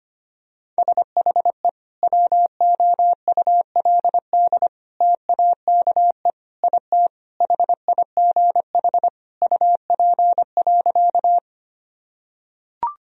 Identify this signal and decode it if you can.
Morse code, heard as a keyed tone, about 25 wpm, 700 Hz. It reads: SHE WOULD TAKE IT HIGH UP.
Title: Sentences from Top 1000 Words - Review Part 1 - 25wpm